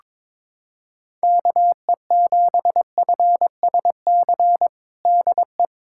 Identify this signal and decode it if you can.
Morse code, heard as a keyed tone, about 22 wpm, 700 Hz. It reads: KE7FSC DE
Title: USA Callsign DE Callsign - 6 Characters - 22wpm